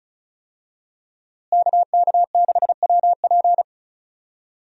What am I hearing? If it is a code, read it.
Morse code, heard as a keyed tone, about 34 wpm, 700 Hz. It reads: KK6WP
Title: USA Call Signs - 5 Characters - Repeated 3x - 35wpm